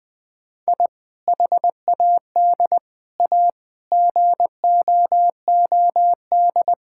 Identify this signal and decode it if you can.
Morse code, heard as a keyed tone, about 19 wpm, 700 Hz. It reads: I HAD A GOOD
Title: Sentences with 3-Word Phrases - 20wpm